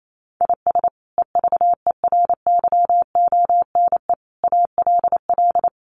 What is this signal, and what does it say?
Morse code, heard as a keyed tone, about 28 wpm, 700 Hz. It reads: IS EVERYONE ALL
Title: Sentences from Top 800 Words 28wpm